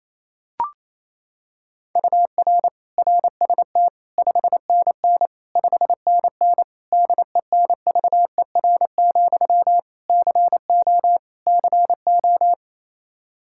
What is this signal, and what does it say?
Morse code, heard as a keyed tone, about 28 wpm, 700 Hz. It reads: UR RST 5NN 5NN DENVER, CO CO
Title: Practice QSO - Exchange 1 - Beginning - 28wpm